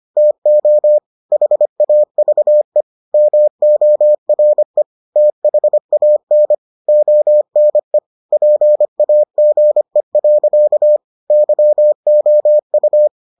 Morse code, heard as a keyed tone, about 25 wpm, 600 Hz. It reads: TO HAVE MORE THAN ONE PAGE. YOU